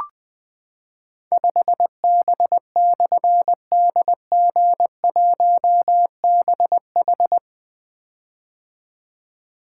Morse code, heard as a keyed tone, about 20 wpm, 700 Hz. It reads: 5B/DG1BH
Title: Difficult Call Signs 20wpm